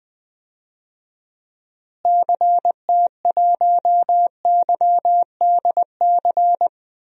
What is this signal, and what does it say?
Morse code, heard as a keyed tone, about 20 wpm, 700 Hz. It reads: CT1YDC
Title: DXCC-100 Call Signs - 6 Characters - Repeated 3x - 20wpm